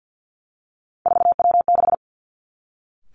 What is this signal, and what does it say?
Morse code, heard as a keyed tone, about 50 wpm, 700 Hz. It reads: 4W6
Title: Three Challenging Letter-Number Combinations 50wpm